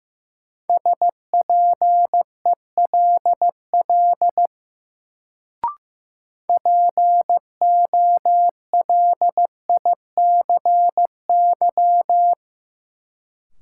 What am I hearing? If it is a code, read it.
Morse code, heard as a keyed tone, about 15 wpm, 700 Hz. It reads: SPELL POLICY
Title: Sentences from Top 1200 Words - New Word Drill Alt - 15wpm